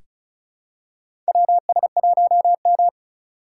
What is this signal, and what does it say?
Morse code, heard as a keyed tone, about 34 wpm, 700 Hz. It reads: WS1M
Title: USA Call Signs - 4 Characters - 4x Character Spacing - 35wpm